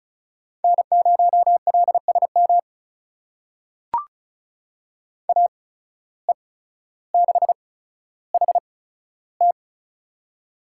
Morse code, heard as a keyed tone, about 36 wpm, spaced as for 15 wpm, 700 Hz. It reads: N0LSM AE6HT
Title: USA Call Signs - 5 Characters - 8x Character Spacing - 35wpm